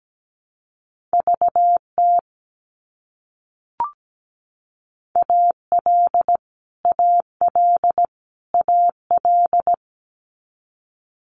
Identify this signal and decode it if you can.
Morse code, heard as a keyed tone, about 17 wpm, 700 Hz. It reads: VT AL AL AL
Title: US States Repeated 3x 17wpm